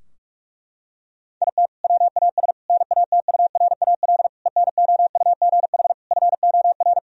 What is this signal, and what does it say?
Morse code, heard as a keyed tone, about 43 wpm, 700 Hz. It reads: IT WAS NATURAL ENOUGH FOR
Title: Sentences from Top 1000 Words - Review Part 4 - 45wpm